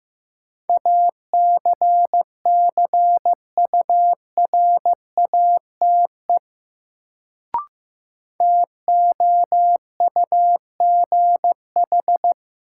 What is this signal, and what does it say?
Morse code, heard as a keyed tone, about 15 wpm, 700 Hz. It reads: ACCURATE TOUGH
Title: Sentences from Top 1200 Words - New Word Drill Alt - 15wpm